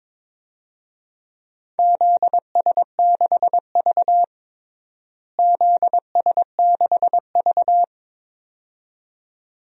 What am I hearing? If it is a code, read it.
Morse code, heard as a keyed tone, about 21 wpm, 700 Hz. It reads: ZS6V ZS6V
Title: DXCC-100 Call Signs - 4 Characters - Repeated 2x - 22wpm